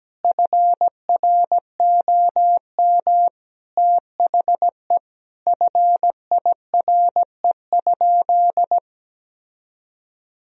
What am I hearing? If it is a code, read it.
Morse code, heard as a keyed tone, about 17 wpm, 700 Hz. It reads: FROM THE FIRE?
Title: Sentences from Top 1000 Words - Review Part 3 - 17wpm